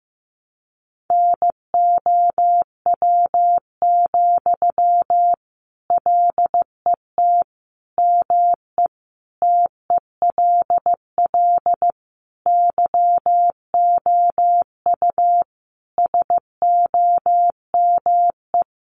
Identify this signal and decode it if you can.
Morse code, heard as a keyed tone, about 15 wpm, 700 Hz. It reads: NOW, LET ME TELL YOU SOME
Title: Sentences from Top 800 Words 15wpm